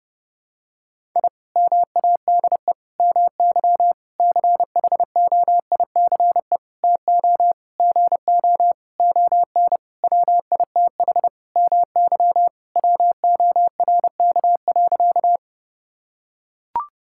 Morse code, heard as a keyed tone, about 32 wpm, 700 Hz. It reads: I MADE MY CHOICE TO GO ON WITH MY WORK.
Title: Sentences from Top 1000 Words - Review Part 6 - 30wpm